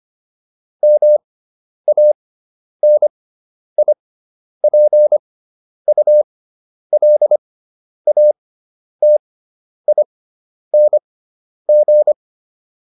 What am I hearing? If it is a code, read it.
Morse code, heard as a keyed tone, about 25 wpm, spaced as for 10 wpm, 600 Hz. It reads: MANIPULATING